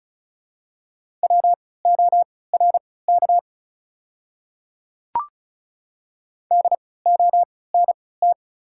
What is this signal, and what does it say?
Morse code, heard as a keyed tone, about 36 wpm, spaced as for 20 wpm, 700 Hz. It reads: WORK DONT
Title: 3x Character Spacing - Top 300 Words - 35wpm